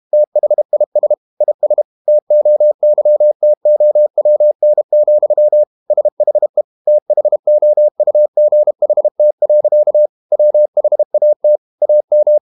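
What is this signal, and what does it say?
Morse code, heard as a keyed tone, about 32 wpm, 600 Hz. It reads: THIS IS TOYTOWN, SHE THOUGHT. WHAT AM
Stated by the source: Morse code practice at 32 wpm from CW med Gnister, reading Ken Follett's novel Never in English.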